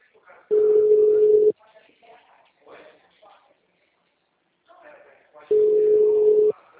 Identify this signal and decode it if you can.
Ringback tone (Europe, 425 Hz), with speech.
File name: out- -2015- - - .22870